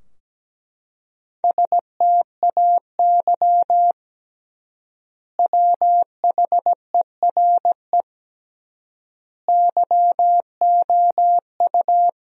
Morse code, heard as a keyed tone, about 17 wpm, 700 Hz. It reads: STAY WHERE YOU
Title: 3x Word-Spacing - Sentences from Top 500 Words - 17wpm